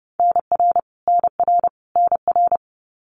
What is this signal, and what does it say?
Morse code, heard as a keyed tone, about 30 wpm, 700 Hz. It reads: NR NR NR
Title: Abbreviations Repeated 3x - 30wpm